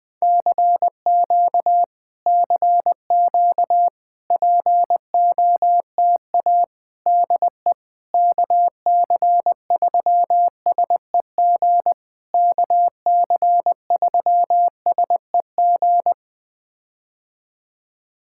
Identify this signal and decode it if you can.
Morse code, heard as a keyed tone, about 20 wpm, 700 Hz. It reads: CQ CQ POTA DE KC3SEG KC3SEG
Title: POTA QSO Practice - 20wpm